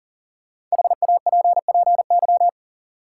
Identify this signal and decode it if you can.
Morse code, heard as a keyed tone, about 40 wpm, 700 Hz. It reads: HAPPY
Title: Sentences from Top 1000 Words - New Word Drill - 40wpm